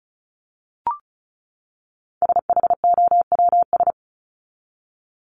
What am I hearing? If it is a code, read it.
Morse code, heard as a keyed tone, about 34 wpm, 700 Hz. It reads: SHOWS
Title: ES:S Suffix - Top 100 Words with Base Words 35wpm